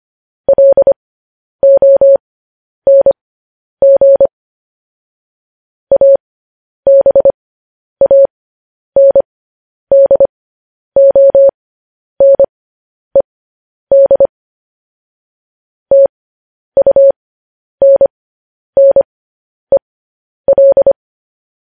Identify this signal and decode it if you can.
Morse code, heard as a keyed tone, about 25 wpm, spaced as for 10 wpm, 550 Hz. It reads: LONG ABANDONED TUNNEL